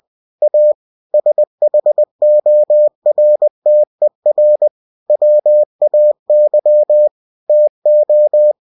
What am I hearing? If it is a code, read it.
Morse code, heard as a keyed tone, about 20 wpm, 600 Hz. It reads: A SHORTER WAY TO
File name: cw-books / winnie-the-pooh / winnie-20wpm-page-0096